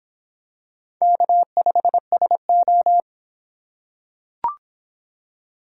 Morse code, heard as a keyed tone, about 26 wpm, 700 Hz. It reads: K5SO